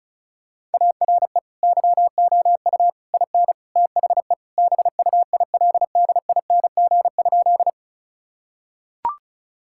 Morse code, heard as a keyed tone, about 36 wpm, 700 Hz. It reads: ARE YOU IN THE BUILDING?
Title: Sentences from Top 1000 Words - Review Part 5 - 35wpm